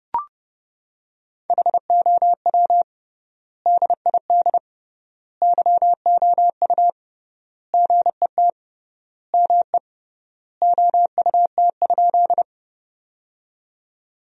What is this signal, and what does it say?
Morse code, heard as a keyed tone, about 32 wpm, 700 Hz. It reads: HOW DID YOU GET ME OUT?
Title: 3x Word-Spacing - Sentences from Top 200 Words - 30wpm